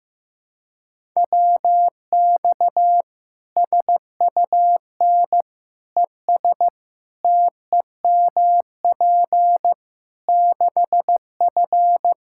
Morse code, heard as a keyed tone, about 15 wpm, 700 Hz. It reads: WX SUN ES TEMP 6F
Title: Practice QSO - Exchange 2 - Info - 15wpm